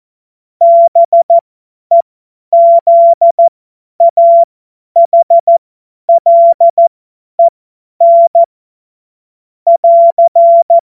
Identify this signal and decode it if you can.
Morse code, heard as a keyed tone, about 14 wpm, spaced as for 10 wpm, 680 Hz. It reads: BEZAHLEN <AR>
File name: Morsekurs / Witze / Witz 18. April 2023 50 F70